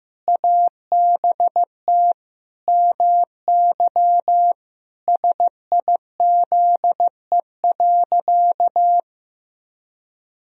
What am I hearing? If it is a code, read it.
Morse code, heard as a keyed tone, about 15 wpm, 700 Hz. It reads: ABT MY SIZE.
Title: Sentences from Top 400 Words and Abbreviations 15wpm